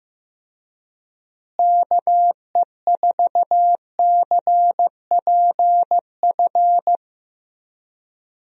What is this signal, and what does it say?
Morse code, heard as a keyed tone, about 15 wpm, 700 Hz. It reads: KE4CPF
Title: USA Call Signs - 6 Characters - Repeated 2x - 15wpm